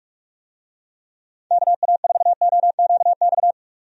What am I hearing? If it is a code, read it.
Morse code, heard as a keyed tone, about 41 wpm, 700 Hz. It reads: KA4OQX
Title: USA Call Signs - 6 Characters - 45wpm